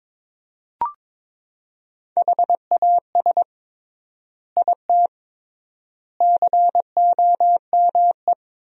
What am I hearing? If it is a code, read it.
Morse code, heard as a keyed tone, about 22 wpm, 700 Hz. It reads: HAS IT COME
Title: 3x Word-Spacing - Sentences from Top 100 Words - 22wpm